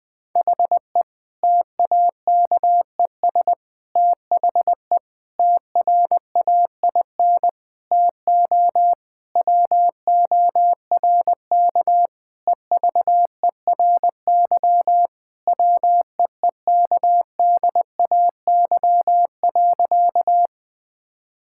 Morse code, heard as a keyed tone, about 20 wpm, 700 Hz. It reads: HE TAKES THE TRAIN TO WORK EVERY WEEKDAY.
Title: Difficult Sentences with Hints - 20wpm